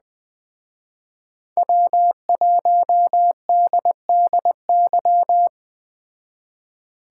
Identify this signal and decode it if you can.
Morse code, heard as a keyed tone, about 20 wpm, 700 Hz. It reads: W1DDY